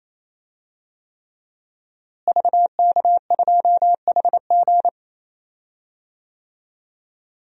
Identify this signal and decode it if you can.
Morse code, heard as a keyed tone, about 28 wpm, 700 Hz. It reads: VK2HG
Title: DXCC-100 Call Signs - 5 Characters - Repeated 3x - 28wpm